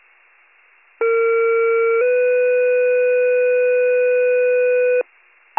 Two-tone page, heard as a one-second tone, then a three-second tone, about 460 then 500 Hz, with noise. It is heard from 1.0 to 5.0 s.